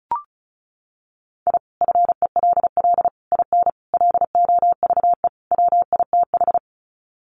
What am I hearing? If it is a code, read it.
Morse code, heard as a keyed tone, about 34 wpm, 700 Hz. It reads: I FELL IN LOVE WITH
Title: Sets of 5 Words Encore 35wpm